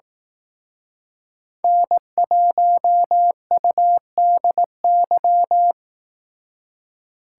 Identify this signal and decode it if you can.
Morse code, heard as a keyed tone, about 18 wpm, 700 Hz. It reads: N1UDY
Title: Speed Racing - USA 1x3 Call Signs 15wpm